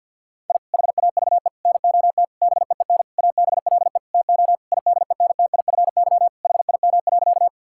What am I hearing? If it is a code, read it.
Morse code, heard as a keyed tone, about 50 wpm, 700 Hz. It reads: I HAVE NOT BEEN ABLE TO IDENTIFY HIM.
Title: Sentences from Top 1000 Words - Review Part 6 - 50wpm